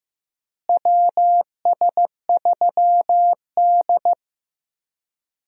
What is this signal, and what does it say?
Morse code, heard as a keyed tone, about 15 wpm, 700 Hz. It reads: WS3D
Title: USA Call Signs - 4 Characters - Repeated 2x - 15wpm